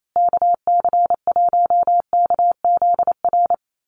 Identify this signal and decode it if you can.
Morse code, heard as a keyed tone, about 28 wpm, 700 Hz. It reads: KC1KZR
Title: USA Call Signs Repeated 2x 28wpm